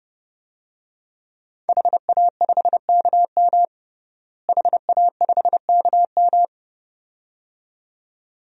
Morse code, heard as a keyed tone, about 30 wpm, 700 Hz. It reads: HA5KM HA5KM
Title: DXCC-100 Call Signs - 5 Characters - Repeated 2x - 30wpm